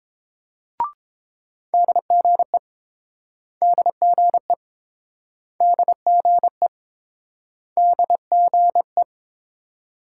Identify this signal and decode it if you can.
Morse code, heard as a keyed tone, about 32 wpm, 700 Hz. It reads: DGE DGE DGE DGE